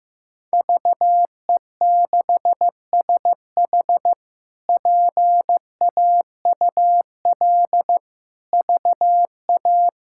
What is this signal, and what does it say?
Morse code, heard as a keyed tone, about 15 wpm, 700 Hz. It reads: VE6SH PAUL VA
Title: K1USN SST QSO Practice - 15wpm